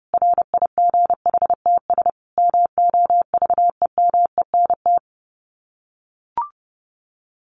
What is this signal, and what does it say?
Morse code, heard as a keyed tone, about 30 wpm, 700 Hz. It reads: RIGHTS MOVEMENT
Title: Sets of 5 Words Encore2 30wpm